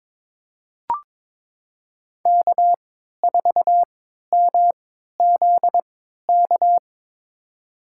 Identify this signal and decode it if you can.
Morse code, heard as a keyed tone, about 22 wpm, spaced as for 12 wpm, 700 Hz. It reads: K4MZK
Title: USA Call Signs - 3x Character Spacing - 22wpm